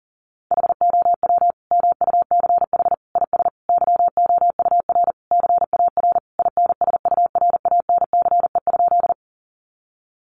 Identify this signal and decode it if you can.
Morse code, heard as a keyed tone, about 40 wpm, 700 Hz. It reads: HOW MUCH IS YOUR CAR INSURANCE?
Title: Sentences from Top 1100 Words 40wpm